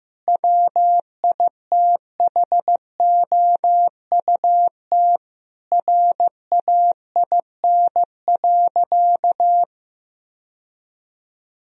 Morse code, heard as a keyed tone, about 15 wpm, 700 Hz. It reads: WITHOUT RAIN.
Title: Sentences from Top 1000 Words - Review Part 3 - 15wpm